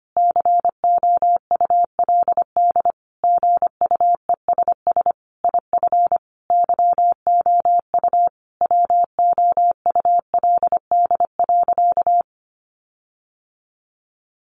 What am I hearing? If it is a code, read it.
Morse code, heard as a keyed tone, about 25 wpm, 700 Hz. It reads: COULD GUESS IF YOU WOULD.
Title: Sentences from Top 1500 Words 25wpm